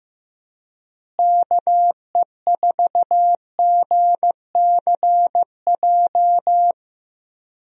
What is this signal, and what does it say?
Morse code, heard as a keyed tone, about 15 wpm, 700 Hz. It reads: KE4GCJ
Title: USA Call Signs - 6 Characters - Repeated 3x - 15wpm